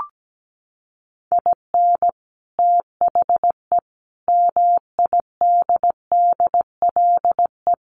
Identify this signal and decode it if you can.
Morse code, heard as a keyed tone, about 17 wpm, 700 Hz. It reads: IN THE MIDDLE
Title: Sets of 5 Words Encore 17wpm